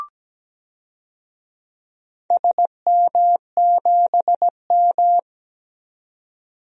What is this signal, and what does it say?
Morse code, heard as a keyed tone, about 17 wpm, 700 Hz. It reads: SM7M
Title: DXCC-100 Call Signs - 4 Characters - 17wpm